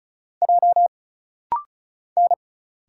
Morse code, heard as a keyed tone, about 36 wpm, 700 Hz. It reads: J N
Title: Single Letter-Number - Rapid-Fire 35wpm